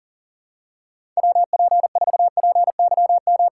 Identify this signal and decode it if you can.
Morse code, heard as a keyed tone, about 40 wpm, 700 Hz. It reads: WP4PYM
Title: USA Callsign DE Callsign - 6 Characters - 40wpm